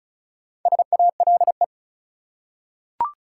Morse code, heard as a keyed tone, about 35 wpm, 700 Hz. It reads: SALE
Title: Four Letter Words - 35wpm